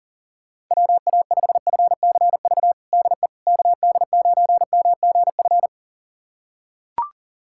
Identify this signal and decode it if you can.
Morse code, heard as a keyed tone, about 37 wpm, 700 Hz. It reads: WA5FCV DE KD9MGF
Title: USA Callsign DE Callsign - 6 Characters - 40wpm